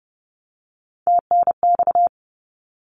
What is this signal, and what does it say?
Morse code, heard as a keyed tone, about 30 wpm, 700 Hz. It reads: TNX
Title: Abbreviations 30wpm